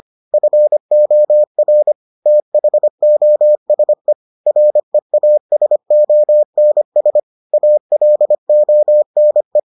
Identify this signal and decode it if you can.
Morse code, heard as a keyed tone, about 25 wpm, 600 Hz. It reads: FOR THOSE REASONS ALONE